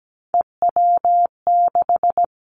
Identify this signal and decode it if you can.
Morse code, heard as a keyed tone, about 17 wpm, 700 Hz. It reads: EW6
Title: Three Challenging Letter-Number Combinations 17wpm